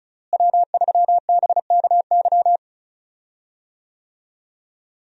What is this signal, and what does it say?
Morse code, heard as a keyed tone, about 36 wpm, 700 Hz. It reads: W3BKY
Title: USA Call Signs - 5 Characters - Repeated 2x - 35wpm